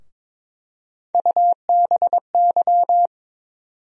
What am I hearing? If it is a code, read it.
Morse code, heard as a keyed tone, about 22 wpm, 700 Hz. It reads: UBY
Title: USA Call Sign Postfixes - 3 Letters 22wpm